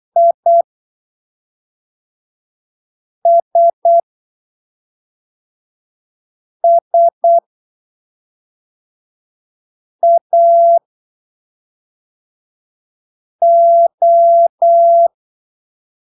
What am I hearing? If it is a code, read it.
Morse code, heard as a keyed tone, about 8 wpm, 680 Hz. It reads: I S S A O
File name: arquivos / texto / texto 18 08ppm